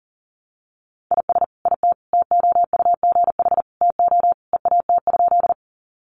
Morse code, heard as a keyed tone, about 40 wpm, 700 Hz. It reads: IS IT TOUGH TO EAT?
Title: Sentences from Top 1200 Words 40wpm